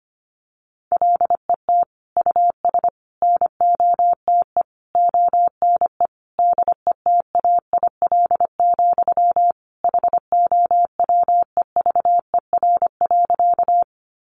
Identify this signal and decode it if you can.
Morse code, heard as a keyed tone, about 25 wpm, 700 Hz. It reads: LET US NOTE ONE DETAIL, HOWEVER.
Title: Sentences from Top 1500 Words 25wpm